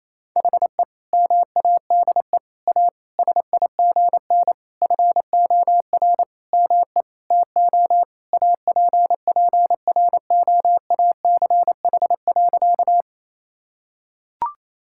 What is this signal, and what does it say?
Morse code, heard as a keyed tone, about 28 wpm, 700 Hz. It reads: HE MADE A SIGN FOR ME TO APPROACH.
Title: Sentences from Top 1000 Words - Review Part 6 - 28wpm